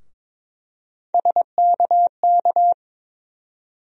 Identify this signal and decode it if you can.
Morse code, heard as a keyed tone, about 22 wpm, 700 Hz. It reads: SKK
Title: USA Call Sign Postfixes - 3 Letters 22wpm